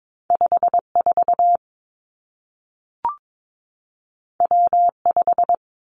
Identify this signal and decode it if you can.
Morse code, heard as a keyed tone, about 22 wpm, 700 Hz. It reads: 5 4 W 5
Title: Two Challenging Letter-Number Combinations 22wpm